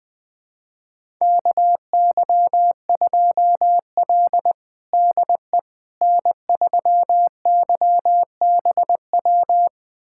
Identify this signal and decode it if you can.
Morse code, heard as a keyed tone, about 20 wpm, 700 Hz. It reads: KY2L DE N3YBW
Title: USA Callsign DE Callsign 20wpm